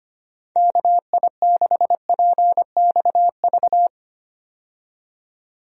Morse code, heard as a keyed tone, about 25 wpm, 700 Hz. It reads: KI6PXV
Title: USA Call Signs - 6 Characters - 25wpm